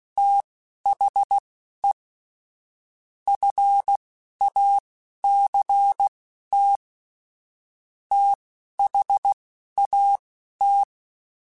Morse code, heard as a keyed tone, about 16 wpm, spaced as for 11 wpm, 800 Hz. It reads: THE FACT THAT
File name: Technical Subjects / Technical Subjects-mp3s / AND Gate @10